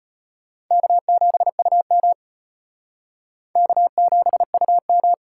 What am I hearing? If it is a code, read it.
Morse code, heard as a keyed tone, about 30 wpm, 700 Hz. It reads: K7UM K7UM
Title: Speed Racing - USA 1x2 Call Signs 25wpm